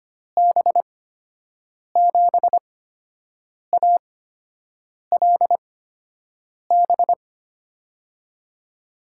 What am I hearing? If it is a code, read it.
Morse code, heard as a keyed tone, about 25 wpm, spaced as for 7 wpm, 700 Hz. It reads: B7ALB
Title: USA Call Signs - 6 Characters - 8x Character Spacing - 25wpm